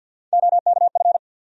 Morse code, heard as a keyed tone, about 47 wpm, 700 Hz. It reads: OKF
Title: USA Call Sign Postfixes - 3 Letters 50wpm